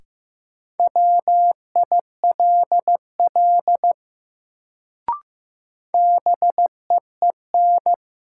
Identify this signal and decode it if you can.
Morse code, heard as a keyed tone, about 15 wpm, 700 Hz. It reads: WILL BEEN
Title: Sentences from Top 100 Words - New Word Drill - 15wpm